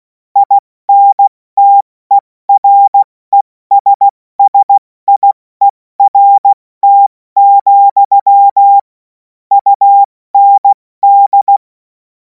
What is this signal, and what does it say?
Morse code, heard as a keyed tone, about 16 wpm, spaced as for 14 wpm, 800 Hz. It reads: INTERESSIERT, UND
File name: Morsekurs / Witze / Witz 24. März 2026 70 F80